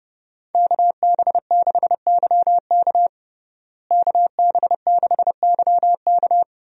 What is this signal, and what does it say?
Morse code, heard as a keyed tone, about 30 wpm, 700 Hz. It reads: KB6YK KB6YK
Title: USA Call Signs - 5 Characters - Repeated 3x - 30wpm